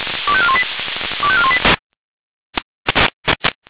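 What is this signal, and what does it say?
Five-tone selective call (ZVEI), heard as quick runs of five tones, tones of about 70 ms. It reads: 25418 25418
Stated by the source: ZVEI five-tone selective call received by SDR.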